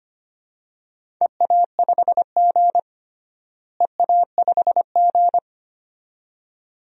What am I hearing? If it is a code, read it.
Morse code, heard as a keyed tone, about 25 wpm, 700 Hz. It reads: EA5G EA5G
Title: DXCC-100 Call Signs - 4 Characters - Repeated 2x - 25wpm